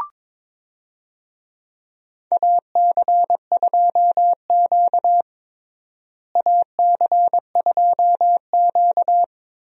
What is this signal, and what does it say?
Morse code, heard as a keyed tone, about 21 wpm, 700 Hz. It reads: AC2Q AC2Q
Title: USA Call Signs - 4 Characters - Repeated 2x - 22wpm